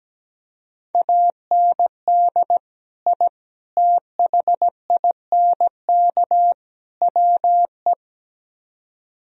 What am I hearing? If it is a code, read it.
Morse code, heard as a keyed tone, about 17 wpm, 700 Hz. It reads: AND I THINK WE
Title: Speed-Racing - Sets of 4 Words 17wpm